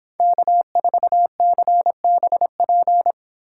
Morse code, heard as a keyed tone, about 26 wpm, 700 Hz. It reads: K4CBP